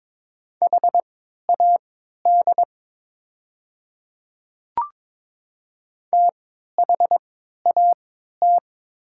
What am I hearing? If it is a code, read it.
Morse code, heard as a keyed tone, about 22 wpm, spaced as for 12 wpm, 700 Hz. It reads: HAD THAT
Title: 3x Character Spacing - Top 200 Words - 22wpm